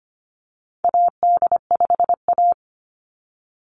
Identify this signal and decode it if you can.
Morse code, heard as a keyed tone, about 25 wpm, 700 Hz. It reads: AD5A
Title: NA SOTA Chaser Callsigns 25wpm